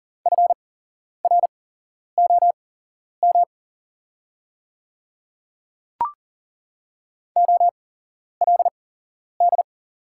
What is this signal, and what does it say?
Morse code, heard as a keyed tone, about 40 wpm, spaced as for 11 wpm, 700 Hz. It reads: FROM OLD
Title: 8x Character Spacing - Top 500 Words - 40wpm